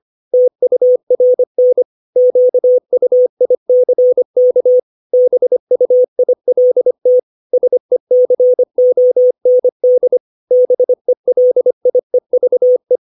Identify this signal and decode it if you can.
Morse code, heard as a keyed tone, about 25 wpm, 500 Hz. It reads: TURN QUICK BUILT SECOND BELIEVE
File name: 632 words / 632 Words 25WPM